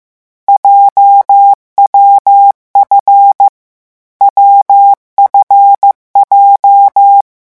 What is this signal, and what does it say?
Morse code, heard as a keyed tone, about 15 wpm, 800 Hz. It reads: JWF WFJ
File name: morse mp3 / lecon7-vitesse 14